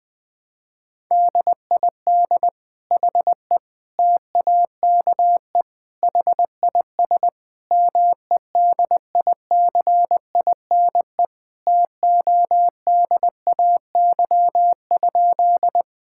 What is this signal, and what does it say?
Morse code, heard as a keyed tone, about 20 wpm, 700 Hz. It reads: DID HE TAKE HIS MEDICINE TODAY?
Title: Sentences from Top 1000 Words - Review Part 6 - 20wpm